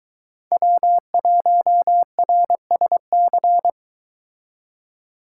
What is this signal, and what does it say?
Morse code, heard as a keyed tone, about 22 wpm, 700 Hz. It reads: W1RSC